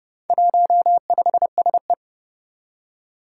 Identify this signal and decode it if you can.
Morse code, heard as a keyed tone, about 30 wpm, 700 Hz. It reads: 15SE
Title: Four Challenging Letter-Number Combinations 30wpm